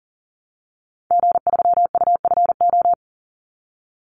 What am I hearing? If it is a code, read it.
Morse code, heard as a keyed tone, about 40 wpm, 700 Hz. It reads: G3UFO
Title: Call Signs Repeated 3x - 40wpm